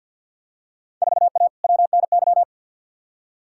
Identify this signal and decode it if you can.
Morse code, heard as a keyed tone, about 53 wpm, 700 Hz. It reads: 4A WNY
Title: Field Day Exchanges Encore - 50wpm